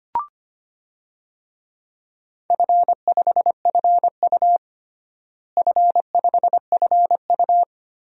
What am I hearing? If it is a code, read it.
Morse code, heard as a keyed tone, about 25 wpm, 700 Hz. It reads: F5FU F5FU
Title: DXCC-100 Call Signs - 4 Characters - Repeated 3x - 25wpm